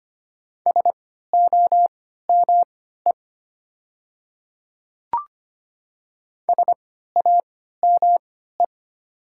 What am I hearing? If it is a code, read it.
Morse code, heard as a keyed tone, about 25 wpm, spaced as for 14 wpm, 690 Hz. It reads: SOME SAME
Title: 3x Character Spacing - Top 300 Words - 25wpm